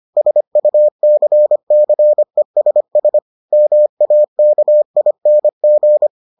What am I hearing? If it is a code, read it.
Morse code, heard as a keyed tone, about 25 wpm, 600 Hz. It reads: SUCCESS MAKING